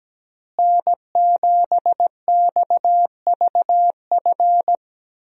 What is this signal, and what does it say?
Morse code, heard as a keyed tone, about 17 wpm, 700 Hz. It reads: N7XVF
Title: USA Call Signs - 5 Characters - Repeated 2x - 17wpm